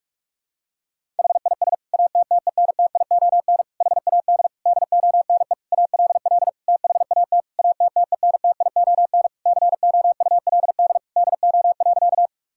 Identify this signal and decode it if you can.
Morse code, heard as a keyed tone, about 41 wpm, 700 Hz. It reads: HIS ATTENTION HAD DONE ALL THAT ATTENTION COULD DO.
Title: Sentences from Top 1000 Words - Review Part 6 - 45wpm